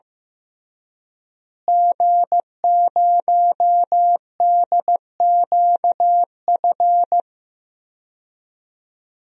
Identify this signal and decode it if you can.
Morse code, heard as a keyed tone, about 15 wpm, 700 Hz. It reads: G0DQF
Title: DXCC-100 Call Signs - 5 Characters - Repeated 3x - 15wpm